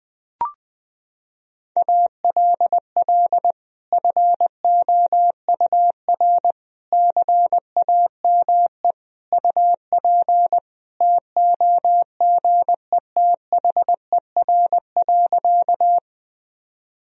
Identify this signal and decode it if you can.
Morse code, heard as a keyed tone, about 20 wpm, 700 Hz. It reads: ALL FOUR CAME UP TOGETHER.
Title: Sentences from Top 1000 Words - Review Part 1 - 20wpm